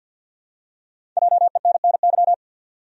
Morse code, heard as a keyed tone, about 51 wpm, 700 Hz. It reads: JENNY
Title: USA Common Names - Max 5 Letters - 50wpm